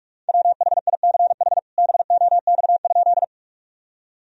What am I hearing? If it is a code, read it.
Morse code, heard as a keyed tone, about 46 wpm, 700 Hz. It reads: WHICH BOX?
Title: Sentences from Top 1000 Words - Review Part 2 - 45wpm